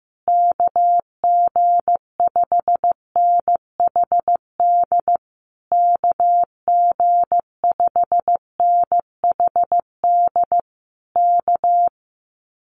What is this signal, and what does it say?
Morse code, heard as a keyed tone, about 15 wpm, 700 Hz. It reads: KG5NHD KG5NHD K
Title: Common QSO Phrases Encore 15wpm